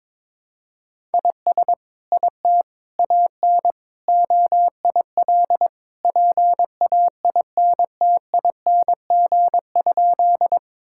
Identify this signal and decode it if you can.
Morse code, heard as a keyed tone, about 22 wpm, 700 Hz. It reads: IS IT AN OIL PAINTING?
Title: Sentences from Top 1000 Words - Review Part 6 - 22wpm